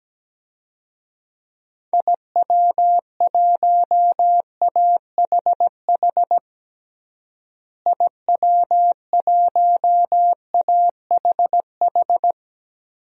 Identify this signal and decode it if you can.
Morse code, heard as a keyed tone, about 17 wpm, 700 Hz. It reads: IW1AHH IW1AHH
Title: DXCC-100 Call Signs - 6 Characters - Repeated 2x - 17wpm